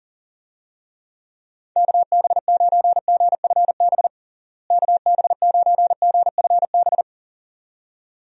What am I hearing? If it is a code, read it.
Morse code, heard as a keyed tone, about 37 wpm, 700 Hz. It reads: KB9GFB KB9GFB
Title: USA Call Signs - 6 Characters - Repeated 2x - 40wpm